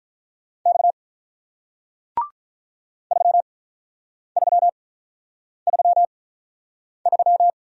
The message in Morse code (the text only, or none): X 3333